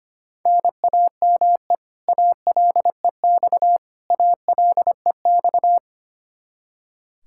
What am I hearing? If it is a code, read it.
Morse code, heard as a keyed tone, about 25 wpm, 700 Hz. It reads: NAME ALEX ALEX
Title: Practice QSO - Exchange 1 - Beginning - 25wpm